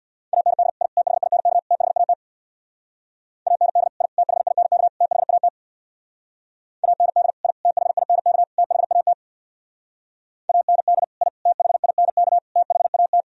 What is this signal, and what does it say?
Morse code, heard as a keyed tone, about 64 wpm, 700 Hz. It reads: AND I THINK THAT AND I THINK THAT AND I THINK THAT AND I THINK THAT